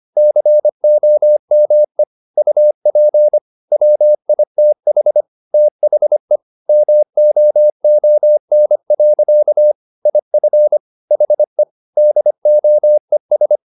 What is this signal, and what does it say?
Morse code, heard as a keyed tone, about 25 wpm, 600 Hz. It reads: COME UP WITH THE MOON. IF HE DOES
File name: the old man and the sea part 1 25wpm